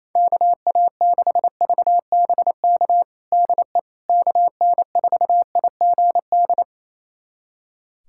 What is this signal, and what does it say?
Morse code, heard as a keyed tone, about 28 wpm, 700 Hz. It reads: KA6VBK DE KN4IGD
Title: USA Callsign DE Callsign 28wpm